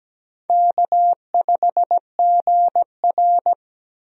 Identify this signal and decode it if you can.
Morse code, heard as a keyed tone, about 17 wpm, 700 Hz. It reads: K5GR
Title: USA Call Signs - 4 Characters - 17wpm